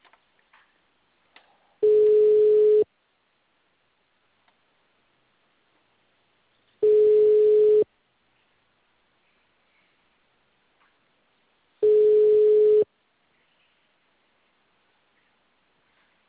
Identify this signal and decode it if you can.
Ringback tone (Europe, 425 Hz).